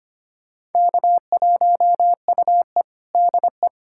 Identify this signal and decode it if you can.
Morse code, heard as a keyed tone, about 25 wpm, 700 Hz. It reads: K1UE DE
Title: USA Callsign DE Callsign - 4 Characters - 25wpm